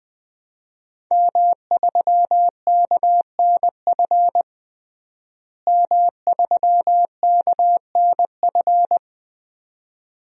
Morse code, heard as a keyed tone, about 21 wpm, 700 Hz. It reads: M3KNF M3KNF
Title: DXCC-100 Call Signs - 5 Characters - Repeated 2x - 20wpm